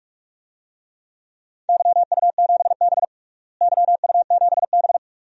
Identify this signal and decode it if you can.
Morse code, heard as a keyed tone, about 46 wpm, 700 Hz. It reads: YU7B YU7B
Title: DXCC-100 Call Signs - 4 Characters - Repeated 2x - 45wpm